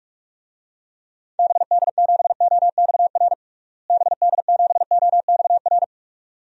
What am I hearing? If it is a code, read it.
Morse code, heard as a keyed tone, about 46 wpm, 700 Hz. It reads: BD7OXR BD7OXR
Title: DXCC-100 Call Signs - 6 Characters - Repeated 2x - 45wpm